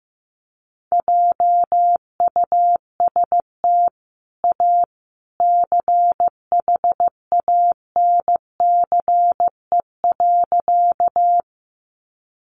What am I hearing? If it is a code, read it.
Morse code, heard as a keyed tone, about 15 wpm, 700 Hz. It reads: JUST A CHANCE.
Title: Sentences from Top 800 Words 15wpm